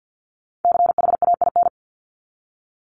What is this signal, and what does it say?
Morse code, heard as a keyed tone, about 50 wpm, 700 Hz. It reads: CHAIN
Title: Top 1251-1500 Words 50wpm